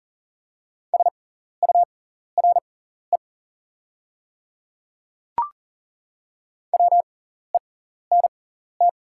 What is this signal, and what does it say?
Morse code, heard as a keyed tone, about 39 wpm, spaced as for 14 wpm, 690 Hz. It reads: SURE WENT
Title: 6x Character Spacing - Top 300 Words - 40wpm